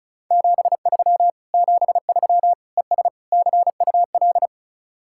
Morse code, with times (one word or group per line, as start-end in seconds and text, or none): 0.3-1.3 73
1.5-2.5 73
2.8-3.1 ES
3.3-4.5 CUL